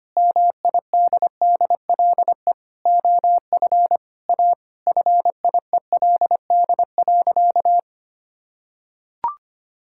Morse code, heard as a keyed tone, about 25 wpm, 700 Hz. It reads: MIDDLE OF A FIELD.
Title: Sentences from Top 1000 Words - Review Part 6 - 25wpm